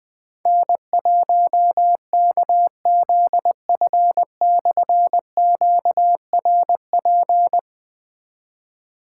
Morse code, heard as a keyed tone, about 21 wpm, 700 Hz. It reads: N1KZF/QRP
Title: Difficult Call Signs 20wpm